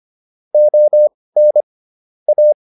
Morse code, heard as a keyed tone, about 25 wpm, spaced as for 18 wpm, 600 Hz. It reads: ON A